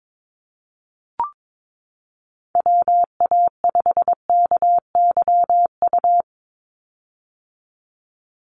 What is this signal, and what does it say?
Morse code, heard as a keyed tone, about 22 wpm, 700 Hz. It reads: WA5KYU
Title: USA Call Signs 22wpm